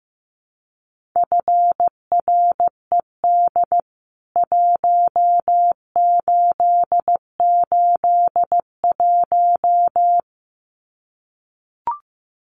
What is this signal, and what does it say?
Morse code, heard as a keyed tone, about 15 wpm, 700 Hz. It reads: FRED 1881
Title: CWT Exchange 15wpm